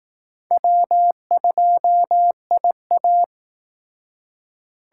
Morse code, heard as a keyed tone, about 19 wpm, 700 Hz. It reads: W2IA
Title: Speed Racing - USA 1x2 Call Signs 15wpm